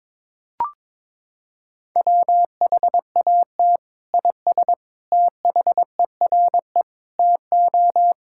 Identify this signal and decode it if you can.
Morse code, heard as a keyed tone, about 22 wpm, 700 Hz. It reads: WHAT IS THERE TO
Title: Sentences from Top 1000 Words - Review Part 5 - 22wpm